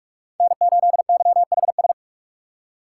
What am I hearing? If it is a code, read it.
Morse code, heard as a keyed tone, about 58 wpm, 700 Hz. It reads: N8YHS